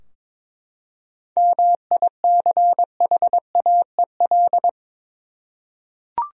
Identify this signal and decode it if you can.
Morse code, heard as a keyed tone, about 22 wpm, 700 Hz. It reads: MICHAEL
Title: USA Common Names - 22wpm